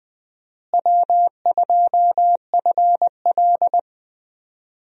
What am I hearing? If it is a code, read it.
Morse code, heard as a keyed tone, about 20 wpm, 700 Hz. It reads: W2FL
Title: USA Call Signs - 4 Characters - Repeated 2x - 20wpm